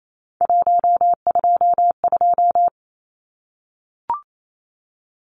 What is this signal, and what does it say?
Morse code, heard as a keyed tone, about 28 wpm, 700 Hz. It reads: 122
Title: Numbers - 3-Digit 28wpm